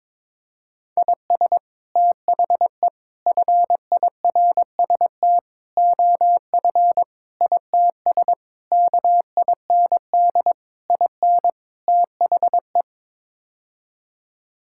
Morse code, heard as a keyed tone, about 22 wpm, 700 Hz. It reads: IS THE FIRST OF ITS KIND IN THE
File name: practice / 22wpm / Sets of 8 Words - 22wpm